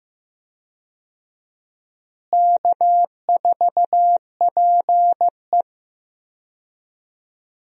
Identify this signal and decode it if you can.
Morse code, heard as a keyed tone, about 15 wpm, 700 Hz. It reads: K4PE
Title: USA Call Signs - 4 Characters - Repeated 3x - 15wpm